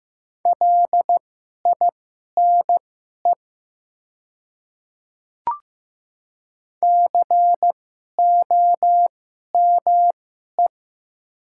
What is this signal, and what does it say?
Morse code, heard as a keyed tone, about 15 wpm, spaced as for 11 wpm, 700 Hz. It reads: LINE COME
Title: 2x Character Spacing - Top 500 Words - 15wpm